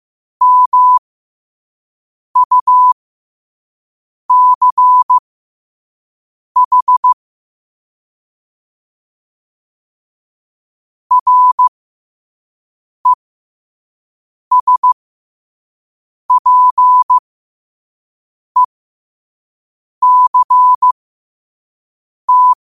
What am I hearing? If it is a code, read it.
Morse code, heard as a keyed tone, about 15 wpm, spaced as for 5 wpm, 1000 Hz. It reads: MUCH RESPECT